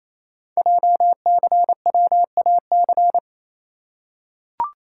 Morse code, heard as a keyed tone, about 27 wpm, 700 Hz. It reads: JCWAC
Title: Code Groups - 5 Letters - 28wpm